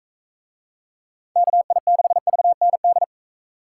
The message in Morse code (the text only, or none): KI6VND